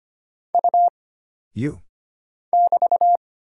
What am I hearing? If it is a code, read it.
Morse code, heard as a keyed tone, about 25 wpm, 700 Hz. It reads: U <BT>
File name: Single Character (Letter:Number:Punctuation:Prosign) - Mind-Melt - 25wpm